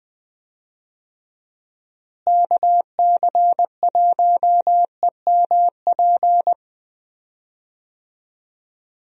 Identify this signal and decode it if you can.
Morse code, heard as a keyed tone, about 21 wpm, 700 Hz. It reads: KC1EMP
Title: USA Call Signs - 6 Characters - Repeated 2x - 20wpm